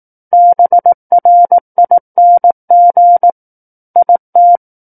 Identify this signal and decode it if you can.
Morse code, heard as a keyed tone, about 18 wpm, 700 Hz. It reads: BRING IT
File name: 18 WPM